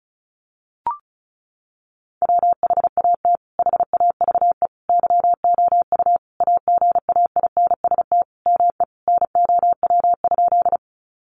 Morse code, heard as a keyed tone, about 34 wpm, 700 Hz. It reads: WHAT HAVE YOU AGAINST ME NOW?
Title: Sentences from Top 400 Words 35wpm